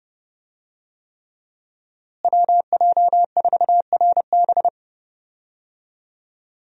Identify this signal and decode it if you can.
Morse code, heard as a keyed tone, about 32 wpm, 700 Hz. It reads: WJ4RB
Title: USA Call Signs - 5 Characters - Repeated 3x - 30wpm